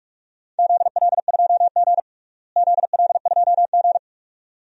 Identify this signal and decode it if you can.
Morse code, heard as a keyed tone, about 46 wpm, 700 Hz. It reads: ZL2G ZL2G
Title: DXCC-100 Call Signs - 4 Characters - Repeated 2x - 45wpm